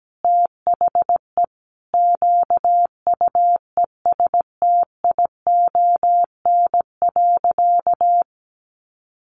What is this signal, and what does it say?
Morse code, heard as a keyed tone, about 17 wpm, 700 Hz. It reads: THE QUESTION.
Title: Sentences from Top 1000 Words 17wpm